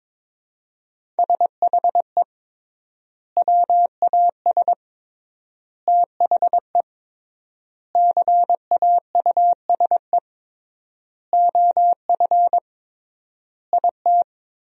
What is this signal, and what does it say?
Morse code, heard as a keyed tone, about 22 wpm, 700 Hz. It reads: SHE WAS THE CAUSE OF IT
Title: 3x Word-Spacing - Sentences from Top 200 Words - 22wpm